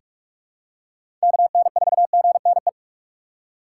Morse code, heard as a keyed tone, about 46 wpm, 700 Hz. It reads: KN4GNE
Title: USA Call Signs - 6 Characters - 3x Character Spacing - 45wpm